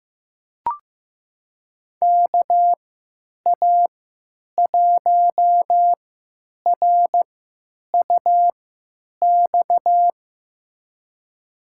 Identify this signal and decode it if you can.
Morse code, heard as a keyed tone, about 15 wpm, spaced as for 9 wpm, 700 Hz. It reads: KA1RUX
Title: USA Call Signs - 6 Characters - 3x Character Spacing - 15wpm